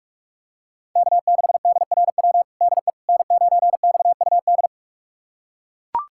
Morse code, heard as a keyed tone, about 42 wpm, 700 Hz. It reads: K6DRW DE N9XUD
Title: USA Callsign DE Callsign - 5 Characters - 45wpm